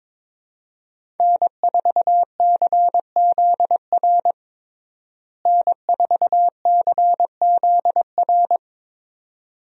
Morse code, heard as a keyed tone, about 22 wpm, 700 Hz. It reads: N4CZR N4CZR
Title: USA Call Signs - 5 Characters - Repeated 2x - 22wpm